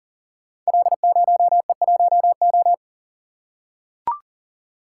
Morse code, heard as a keyed tone, about 42 wpm, 700 Hz. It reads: L0E1O
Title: Code Groups - 5 Letter:Numbers - 40wpm